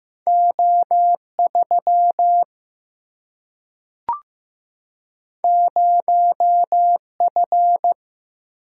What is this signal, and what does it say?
Morse code, heard as a keyed tone, about 15 wpm, 700 Hz. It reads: O 3 0 F
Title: Single-Letter Single-Number 15wpm